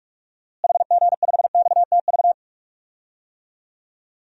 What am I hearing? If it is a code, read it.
Morse code, heard as a keyed tone, about 42 wpm, 700 Hz. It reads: HG5XTV
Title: DXCC-100 Call Signs - 6 Characters - Repeated 2x - 45wpm